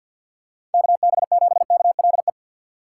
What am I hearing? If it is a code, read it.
Morse code, heard as a keyed tone, about 47 wpm, 700 Hz. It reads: KB7KLE
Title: USA Call Signs - 6 Characters - 3x Character Spacing - 50wpm